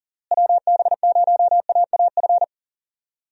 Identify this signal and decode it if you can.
Morse code, heard as a keyed tone, about 40 wpm, 700 Hz. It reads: WB0AAF
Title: USA Call Signs - 6 Characters - Repeated 3x - 40wpm